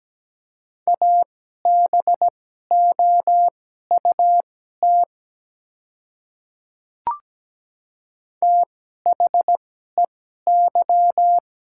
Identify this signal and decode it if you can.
Morse code, heard as a keyed tone, about 17 wpm, spaced as for 12 wpm, 700 Hz. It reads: ABOUT THEY
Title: 2x Character Spacing - Top 300 Words - 17wpm